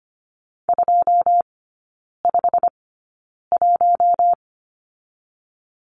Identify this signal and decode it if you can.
Morse code, heard as a keyed tone, about 25 wpm, spaced as for 9 wpm, 690 Hz. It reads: 251
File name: Lesson 15e - Number 5 - Three Characters 25wpm